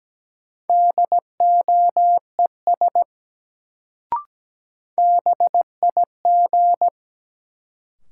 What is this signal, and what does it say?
Morse code, heard as a keyed tone, about 17 wpm, 700 Hz. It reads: DOES BIG
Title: Sentences from Top 200 Words - New Word Drill - 17wpm